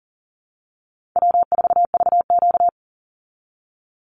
Morse code, heard as a keyed tone, about 37 wpm, 700 Hz. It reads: W4VQ
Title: Call Signs Repeated 2x - 40wpm